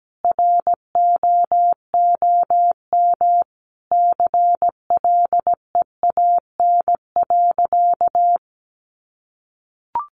Morse code, heard as a keyed tone, about 17 wpm, 700 Hz. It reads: ROOM CLEAN.
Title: Sentences from Top 1100 Words 17wpm